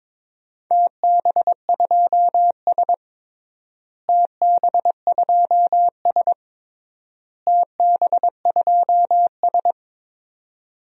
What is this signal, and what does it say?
Morse code, heard as a keyed tone, about 22 wpm, 700 Hz. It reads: TB2S TB2S TB2S
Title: DXCC-100 Call Signs - 4 Characters - Repeated 3x - 22wpm